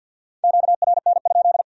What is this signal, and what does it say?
Morse code, heard as a keyed tone, about 61 wpm, 700 Hz. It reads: QRN?
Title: Speed-Racing Q-Codes - 50wpm